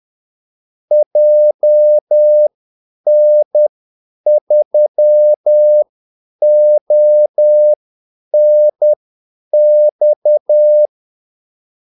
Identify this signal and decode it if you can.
Morse code, heard as a keyed tone, about 10 wpm, spaced as for 8 wpm, 600 Hz. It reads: JN3ONX
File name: Callsigns-10-8wpm